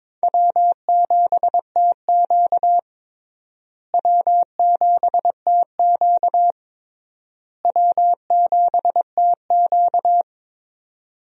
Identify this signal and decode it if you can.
Morse code, heard as a keyed tone, about 22 wpm, 700 Hz. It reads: W7TQ W7TQ W7TQ
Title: USA Call Signs - 4 Characters - Repeated 3x - 22wpm